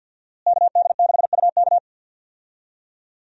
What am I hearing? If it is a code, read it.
Morse code, heard as a keyed tone, about 51 wpm, 700 Hz. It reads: KD6UK
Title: USA Call Signs - 5 Characters - Repeated 2x - 50wpm